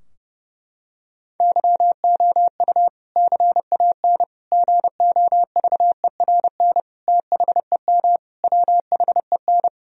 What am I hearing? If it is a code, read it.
Morse code, heard as a keyed tone, about 30 wpm, 700 Hz. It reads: YOU CAN GOVERN THEM WHEN
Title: Sentences from Top 1000 Words - Review Part 2 - 30wpm